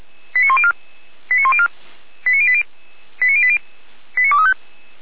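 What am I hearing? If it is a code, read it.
Five-tone selective call (ZVEI), heard as quick runs of five tones, tones of about 80 ms. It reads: 89104 89104 89080 89080 89235
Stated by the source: ZVEI five-tone selective call received by SDR.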